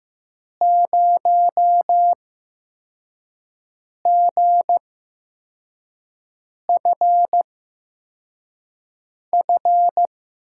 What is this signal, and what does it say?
Morse code, heard as a keyed tone, about 15 wpm, spaced as for 4 wpm, 700 Hz. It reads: 0GFF
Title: USA Call Signs - 6 Characters - 8x Character Spacing - 15wpm